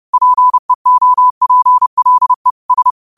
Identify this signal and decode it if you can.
Morse code, heard as a keyed tone, about 30 wpm, 1000 Hz. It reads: PEOPLE S
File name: qotdfiles / 30wpm